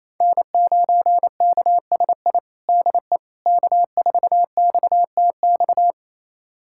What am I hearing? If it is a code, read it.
Morse code, heard as a keyed tone, about 28 wpm, 700 Hz. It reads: N9KSI DE K4XTX
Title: USA Callsign DE Callsign - 5 Characters - 28wpm